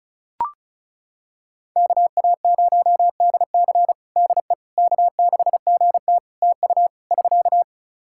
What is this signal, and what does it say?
Morse code, heard as a keyed tone, about 36 wpm, 700 Hz. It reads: KA0DC DE K6GT TU <SK>
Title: Practice QSO - Exchange 4 - Ending - 35wpm